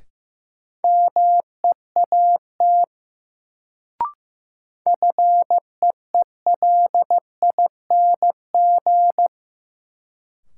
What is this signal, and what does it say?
Morse code, heard as a keyed tone, about 15 wpm, 700 Hz. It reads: MEAT FEELING
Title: Sentences from Top 1000 Words - Review Part 3 - New Word Drill - 15wpm